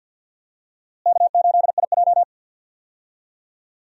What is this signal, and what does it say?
Morse code, heard as a keyed tone, about 51 wpm, 700 Hz. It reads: K8IJ
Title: USA Call Signs - 4 Characters - Repeated 3x - 50wpm